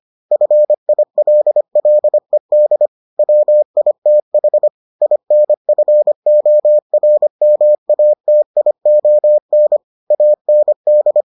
Morse code, heard as a keyed tone, about 25 wpm, 600 Hz. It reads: FILLED WITH INFORMATION AND